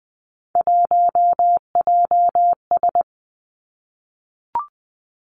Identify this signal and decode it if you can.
Morse code, heard as a keyed tone, about 20 wpm, 700 Hz. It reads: 1JS
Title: Three Challenging Letter-Number Combinations 20wpm